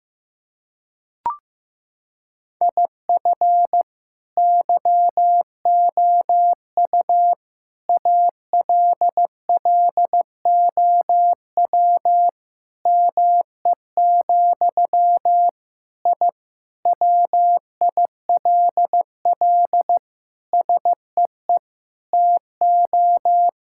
Morse code, heard as a keyed tone, about 15 wpm, 700 Hz. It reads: IF YOU ALLOW ME, I WILL SEE TO
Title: Sentences from Top 1000 Words - Review Part 5 - 15wpm